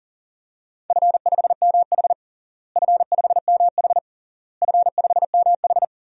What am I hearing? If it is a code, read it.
Morse code, heard as a keyed tone, about 40 wpm, 700 Hz. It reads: F5MH F5MH F5MH
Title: DXCC-100 Call Signs - 4 Characters - Repeated 3x - 40wpm